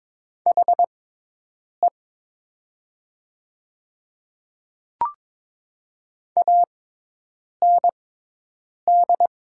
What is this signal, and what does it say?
Morse code, heard as a keyed tone, about 22 wpm, spaced as for 8 wpm, 700 Hz. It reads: HE AND
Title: 6x Character Spacing - Top 500 Words - 22wpm